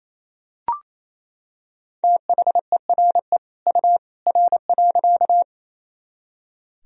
Morse code, heard as a keyed tone, about 28 wpm, 700 Hz. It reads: THERE U R.
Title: Sentences from Top 100 Words and Abbreviations 28wpm